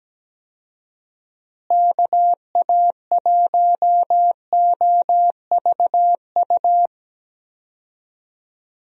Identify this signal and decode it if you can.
Morse code, heard as a keyed tone, about 17 wpm, 700 Hz. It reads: KA1OVU
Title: USA Call Signs - 6 Characters - Repeated 3x - 17wpm